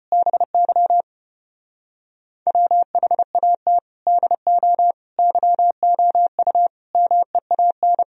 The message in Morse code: BY WHAT DO YOU MEAN

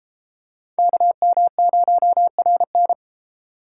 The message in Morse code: KM0RN

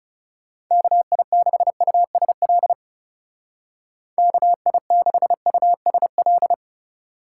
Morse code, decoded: KI6USL KI6USL

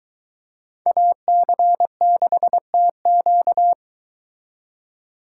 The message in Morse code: AC6TQ

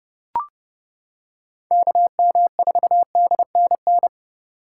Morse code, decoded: KM4DNN

